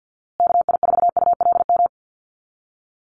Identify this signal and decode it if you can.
Morse code, heard as a keyed tone, about 50 wpm, 700 Hz. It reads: KI4ULM
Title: USA Call Signs Repeated 3x 50wpm